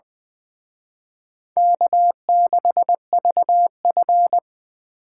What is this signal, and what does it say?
Morse code, heard as a keyed tone, about 23 wpm, 700 Hz. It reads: K6VF
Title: Speed Racing - USA 1x2 Call Signs 20wpm